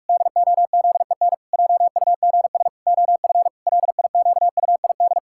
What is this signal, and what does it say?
Morse code, heard as a keyed tone, about 41 wpm, 700 Hz. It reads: DOZEN JUGS OF LIQUID
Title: Pangram Challenge v2 - 45wpm